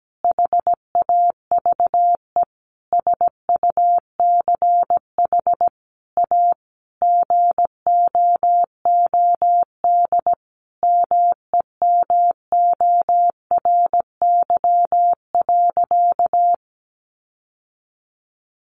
Morse code, heard as a keyed tone, about 17 wpm, 700 Hz. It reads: HAVE SUCH A GOOD MEMORY.
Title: Sentences from Top 1300 Words 17wpm